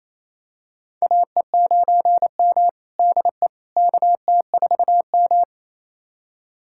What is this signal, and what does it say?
Morse code, heard as a keyed tone, about 28 wpm, 700 Hz. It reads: AE9M DE KT4M
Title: USA Callsign DE Callsign - 4 Characters - 28wpm